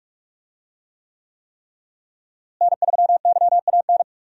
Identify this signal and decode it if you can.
Morse code, heard as a keyed tone, about 46 wpm, 700 Hz. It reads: N3YAN
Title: USA Call Signs - 5 Characters - Repeated 3x - 45wpm